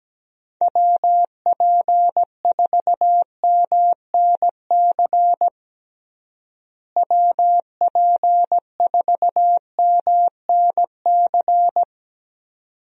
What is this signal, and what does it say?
Morse code, heard as a keyed tone, about 17 wpm, 700 Hz. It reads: WP4MNC WP4MNC
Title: USA Call Signs - 6 Characters - Repeated 2x - 17wpm